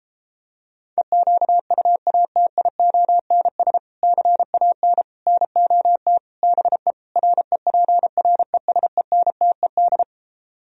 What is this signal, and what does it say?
Morse code, heard as a keyed tone, about 29 wpm, 700 Hz. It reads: EQUATIONS CAN NOT BE REPRESENTED